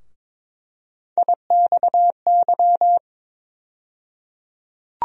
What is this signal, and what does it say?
Morse code, heard as a keyed tone, about 22 wpm, 700 Hz. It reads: IXY
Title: USA Call Sign Postfixes - 3 Letters 22wpm